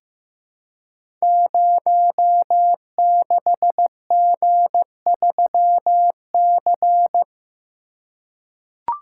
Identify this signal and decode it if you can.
Morse code, heard as a keyed tone, about 15 wpm, 700 Hz. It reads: 06G3C
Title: Code Groups - 5 Letter:Numbers - 15wpm